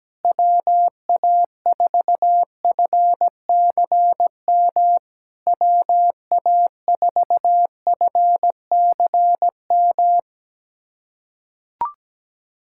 Morse code, heard as a keyed tone, about 17 wpm, 690 Hz. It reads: WA4FCM WA4FCM
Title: SOTA QSO Practice 17wpm